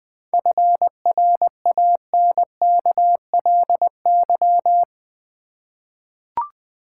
Morse code, heard as a keyed tone, about 20 wpm, 700 Hz. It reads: FRANKLY
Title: 7 Letter Words - First 3 Letters Given - 20wpm